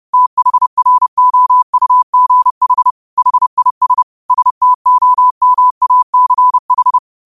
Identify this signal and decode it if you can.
Morse code, heard as a keyed tone, about 30 wpm, 1000 Hz. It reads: THROUGH HIS STOMACH
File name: qotdfiles / 30wpm